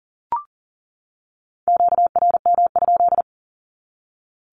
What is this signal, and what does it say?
Morse code, heard as a keyed tone, about 40 wpm, 700 Hz. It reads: QRM?
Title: Q-Codes 40wpm